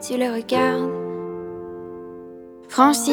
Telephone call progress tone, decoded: dial tone